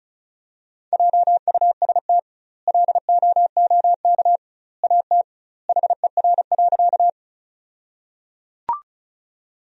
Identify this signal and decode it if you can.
Morse code, heard as a keyed tone, about 34 wpm, 700 Hz. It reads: JUST LOOK AT HER.
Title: 2x Word-Spacing - Sentences from Top 200 Words - 35wpm